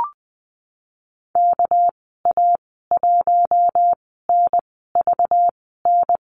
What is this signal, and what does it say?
Morse code, heard as a keyed tone, about 19 wpm, spaced as for 14 wpm, 700 Hz. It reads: KA1NVN
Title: 207 Lesson 33g - Letter j - Call Signs 20wpm